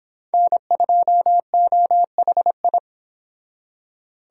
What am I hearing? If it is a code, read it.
Morse code, heard as a keyed tone, about 25 wpm, 700 Hz. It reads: N2OHI